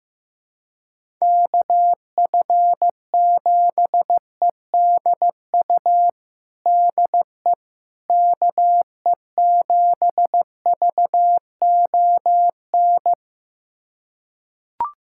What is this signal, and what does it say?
Morse code, heard as a keyed tone, about 16 wpm, 700 Hz. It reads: KF7EDU DE KE7VON
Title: USA Callsign DE Callsign - 6 Characters - 15wpm